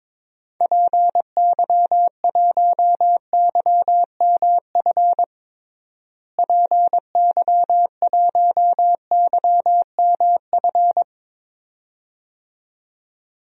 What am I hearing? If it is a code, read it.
Morse code, heard as a keyed tone, about 22 wpm, 700 Hz. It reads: PY1YMF PY1YMF
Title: DXCC-100 Call Signs - 6 Characters - Repeated 2x - 22wpm